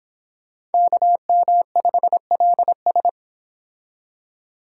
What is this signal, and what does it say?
Morse code, heard as a keyed tone, about 30 wpm, 700 Hz. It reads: KM5LS